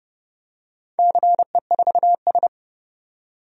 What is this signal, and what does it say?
Morse code, heard as a keyed tone, about 30 wpm, 700 Hz. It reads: CE4S
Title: DXCC-100 Call Signs - 4 Characters - 30wpm